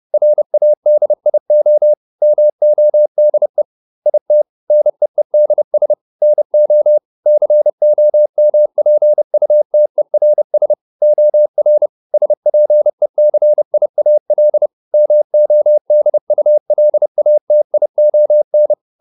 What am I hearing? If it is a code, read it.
Morse code, heard as a keyed tone, about 30 wpm, 600 Hz. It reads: RADIO MODE IT NEEDS NO COMPUTERS OR SPECIAL MODULATION